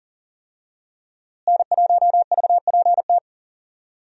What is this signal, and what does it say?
Morse code, heard as a keyed tone, about 39 wpm, 700 Hz. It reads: N1VPT
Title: USA Call Signs - 5 Characters - Repeated 3x - 40wpm